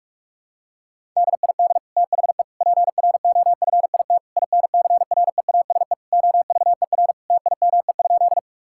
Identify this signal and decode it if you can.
Morse code, heard as a keyed tone, about 42 wpm, 700 Hz. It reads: DID THE PROFIT INCREASE OVER TIME?
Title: Sentences from Top 1000 Words - Review Part 5 - 45wpm